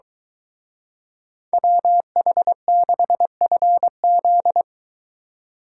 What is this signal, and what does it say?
Morse code, heard as a keyed tone, about 23 wpm, 700 Hz. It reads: WH6FZ